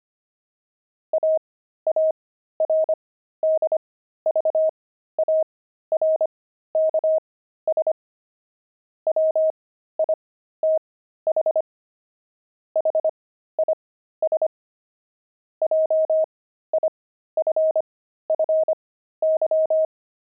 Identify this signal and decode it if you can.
Morse code, heard as a keyed tone, about 25 wpm, spaced as for 13 wpm, 630 Hz. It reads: AARDVARKS WITH HIS JIFFY